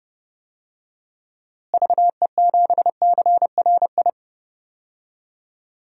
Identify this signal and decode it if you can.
Morse code, heard as a keyed tone, about 30 wpm, 700 Hz. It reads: VE7CRI
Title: DXCC-100 Call Signs - 6 Characters - Repeated 3x - 30wpm